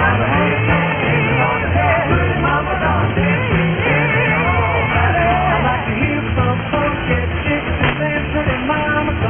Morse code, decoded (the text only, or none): none